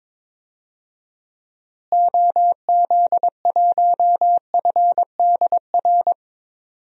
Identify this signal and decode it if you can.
Morse code, heard as a keyed tone, about 22 wpm, 700 Hz. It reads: OZ1FDR
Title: DXCC-100 Call Signs - 6 Characters - Repeated 2x - 22wpm